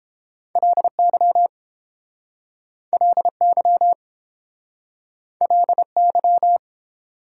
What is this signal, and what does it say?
Morse code, heard as a keyed tone, about 28 wpm, 690 Hz. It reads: LY LY LY